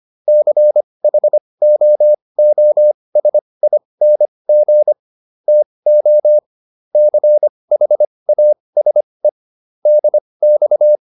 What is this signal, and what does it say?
Morse code, heard as a keyed tone, about 25 wpm, spaced as for 20 wpm, 600 Hz. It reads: CHOOSING TO CHASE DX
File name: SS206 20